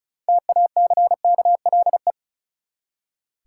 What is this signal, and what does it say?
Morse code, heard as a keyed tone, about 36 wpm, 700 Hz. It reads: TACKLE
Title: Six Letter Words - 35wpm